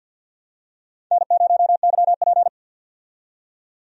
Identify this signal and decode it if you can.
Morse code, heard as a keyed tone, about 53 wpm, 700 Hz. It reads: N0YP
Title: USA Call Signs - 4 Characters - 50wpm